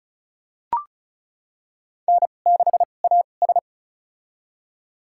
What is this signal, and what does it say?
Morse code, heard as a keyed tone, about 34 wpm, spaced as for 25 wpm, 700 Hz. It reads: N6AS
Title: USA Call Signs - 4 Characters - 2x Character Spacing - 35wpm